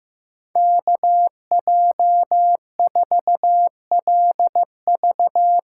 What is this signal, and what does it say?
Morse code, heard as a keyed tone, about 15 wpm, 700 Hz. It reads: KJ4LV
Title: USA Call Signs - 6 Characters - 8x Character Spacing - 15wpm